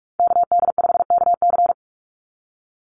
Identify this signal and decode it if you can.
Morse code, heard as a keyed tone, about 44 wpm, 700 Hz. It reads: KD5KC
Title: NA SOTA Activator Callsigns 45wpm